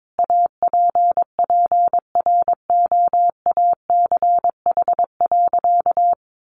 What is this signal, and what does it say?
Morse code, heard as a keyed tone, about 22 wpm, 700 Hz. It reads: APPROACH.
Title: Sentences from Top 1400 Words 22wpm